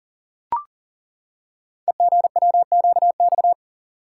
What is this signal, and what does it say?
Morse code, heard as a keyed tone, about 39 wpm, 700 Hz. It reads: EGWQX
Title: Code Groups - 5 Letters - 40wpm